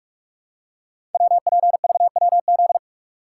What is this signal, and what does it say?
Morse code, heard as a keyed tone, about 44 wpm, 700 Hz. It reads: WPVWZ
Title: Code Groups - 5 Letters - 45wpm